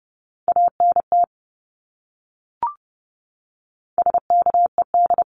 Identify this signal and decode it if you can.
Morse code, heard as a keyed tone, about 30 wpm, 700 Hz. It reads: ANT SKED
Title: Abbreviations Repeated 3x - 30wpm